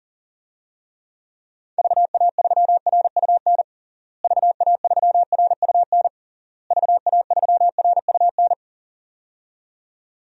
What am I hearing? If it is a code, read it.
Morse code, heard as a keyed tone, about 40 wpm, 700 Hz. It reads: VA3RUN VA3RUN VA3RUN
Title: DXCC-100 Call Signs - 6 Characters - Repeated 3x - 40wpm